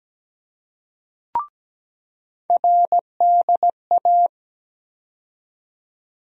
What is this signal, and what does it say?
Morse code, heard as a keyed tone, about 17 wpm, 700 Hz. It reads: RDA
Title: USA Call Sign Postfixes - 3 Letters 17wpm